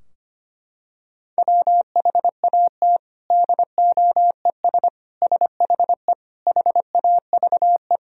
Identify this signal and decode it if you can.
Morse code, heard as a keyed tone, about 25 wpm, 700 Hz. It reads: WHAT DOES SHE HAVE
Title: Sentences from Top 1000 Words - Review Part 3 - 25wpm